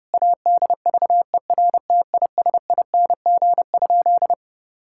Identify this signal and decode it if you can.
Morse code, heard as a keyed tone, about 32 wpm, 700 Hz. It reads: ADVERTISING?
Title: Sentences from Top 1000 Words - Review Part 6 - 30wpm